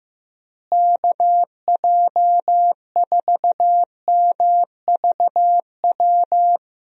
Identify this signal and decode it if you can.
Morse code, heard as a keyed tone, about 16 wpm, 700 Hz. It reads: KJ4MVW
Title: USA Call Signs - 6 Characters - Repeated 3x - 15wpm